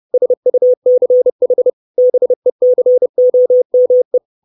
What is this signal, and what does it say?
Morse code, heard as a keyed tone, about 30 wpm, 500 Hz. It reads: SUCH BECOME
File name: 632 Words 30WPM